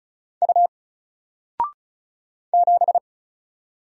Morse code, heard as a keyed tone, about 36 wpm, 700 Hz. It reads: U 7
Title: Single Character (Letter:Number:Punctuation:Prosign) Alt 35wpm